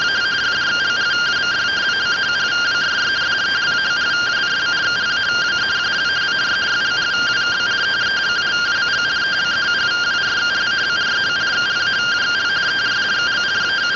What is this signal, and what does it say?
Two-tone teleprinter signal, about 170 Hz shift.